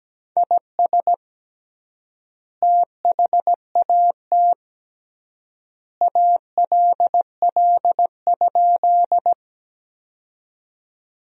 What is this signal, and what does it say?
Morse code, heard as a keyed tone, about 17 wpm, 700 Hz. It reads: IS THAT ALL?
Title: 3x Word-Spacing - Sentences from Top 100 Words - 17wpm